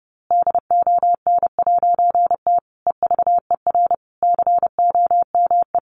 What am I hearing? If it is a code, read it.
Morse code, heard as a keyed tone, about 30 wpm, 700 Hz. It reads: DON'T EVER COME
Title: A Sound Of Thunder 30wpm